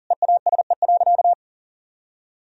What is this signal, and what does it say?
Morse code, heard as a keyed tone, about 40 wpm, 700 Hz. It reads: EASE.
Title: Sentences from Top 1000 Words - Review Part 1 - 40wpm